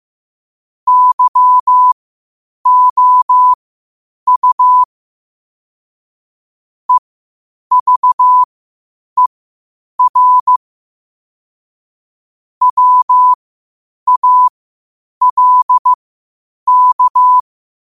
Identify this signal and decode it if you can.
Morse code, heard as a keyed tone, about 15 wpm, spaced as for 8 wpm, 1000 Hz. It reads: YOU EVER WALK
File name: qotdfiles / 8wpm